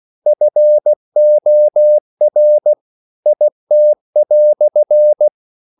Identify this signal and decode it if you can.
Morse code, heard as a keyed tone, about 16 wpm, 600 Hz. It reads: FOR IT"
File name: CW / Practice / EnglishQuotations0216